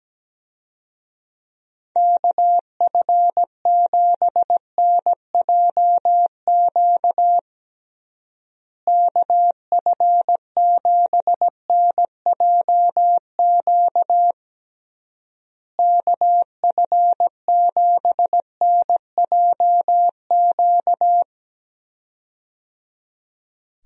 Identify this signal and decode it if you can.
Morse code, heard as a keyed tone, about 17 wpm, 700 Hz. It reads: KF7NJQ KF7NJQ KF7NJQ
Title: USA Call Signs - 6 Characters - Repeated 3x - 17wpm